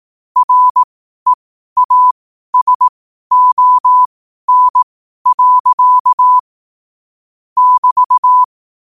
Morse code, heard as a keyed tone, about 18 wpm, spaced as for 12 wpm, 1000 Hz. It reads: REASON. <BT>
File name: qotdfiles / 12wpm